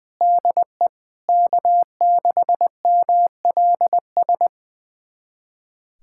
Morse code, heard as a keyed tone, about 20 wpm, 700 Hz. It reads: DE K6MLS
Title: USA Callsign DE Callsign 20wpm